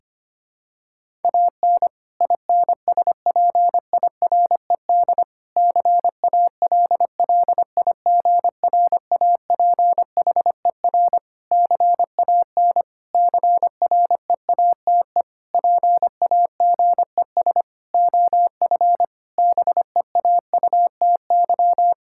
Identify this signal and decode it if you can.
Morse code, heard as a keyed tone, about 25 wpm, 700 Hz. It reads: AN INSPIRED CALLIGRAPHER CAN CREATE PAGES OF BEAUTY
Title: Pangram Challenge v2 - 25wpm